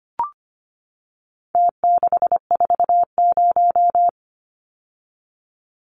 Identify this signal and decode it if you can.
Morse code, heard as a keyed tone, about 25 wpm, 700 Hz. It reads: T640
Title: Numbers - Contest Style - 4-Digit 25wpm